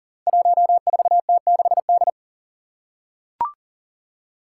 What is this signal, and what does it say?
Morse code, heard as a keyed tone, about 40 wpm, 700 Hz. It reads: 14T6D
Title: Five Challenging Letter-Number Combinations 40wpm